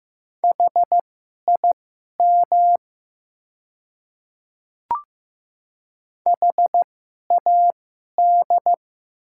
Morse code, heard as a keyed tone, about 15 wpm, spaced as for 11 wpm, 700 Hz. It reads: HIM HAD
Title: 2x Character Spacing - Top 200 Words - 15wpm